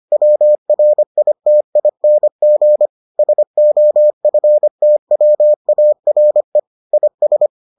Morse code, heard as a keyed tone, about 25 wpm, 600 Hz. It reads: WRITING SOFTWARE IS